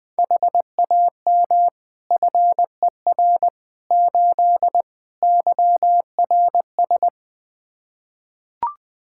Morse code, heard as a keyed tone, about 21 wpm, 700 Hz. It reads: HAM FER 8 YRS
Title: Practice QSO - Exchange 3 - More Info - 20wpm